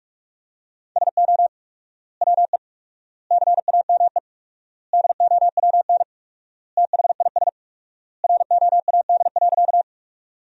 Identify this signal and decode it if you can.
Morse code, heard as a keyed tone, about 43 wpm, 700 Hz. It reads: SO WE CAME DOWN THIS ROAD.
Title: 4x Word-Spacing - Sentences from Top 400 Words - 45wpm